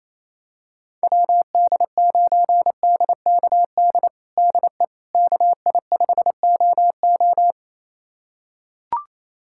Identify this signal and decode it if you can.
Morse code, heard as a keyed tone, about 28 wpm, 700 Hz. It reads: WD9DKD DE KI5OO
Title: USA Callsign DE Callsign 28wpm